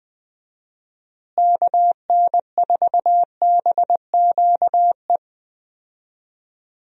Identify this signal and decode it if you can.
Morse code, heard as a keyed tone, about 20 wpm, 700 Hz. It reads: KN4BQE
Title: USA Call Signs - 6 Characters - Repeated 2x - 20wpm